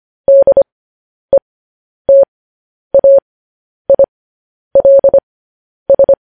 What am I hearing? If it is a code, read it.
Morse code, heard as a keyed tone, about 25 wpm, spaced as for 10 wpm, 550 Hz. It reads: DETAILS